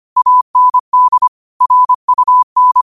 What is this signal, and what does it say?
Morse code, heard as a keyed tone, about 25 wpm, 1000 Hz. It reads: AND RUN
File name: qotdfiles / 25wpm